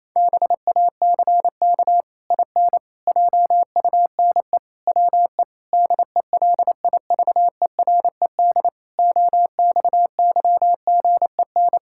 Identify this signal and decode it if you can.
Morse code, heard as a keyed tone, about 28 wpm, 700 Hz. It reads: BACK IN JUNE WE DELIVERED OXYGEN
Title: Pangram Challenge v2 - 28wpm